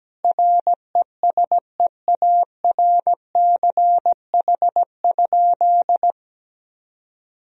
Morse code, heard as a keyed tone, about 17 wpm, 700 Hz. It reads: RESEARCH?
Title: Sentences from Top 1000 Words - Review Part 4 - 17wpm